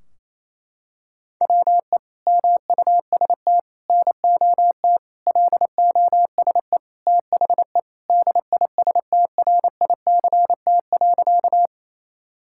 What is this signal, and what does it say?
Morse code, heard as a keyed tone, about 28 wpm, 700 Hz. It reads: WE MUST NOT LOSE THE DISTRICT.
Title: Sentences from Top 1600 Words 28wpm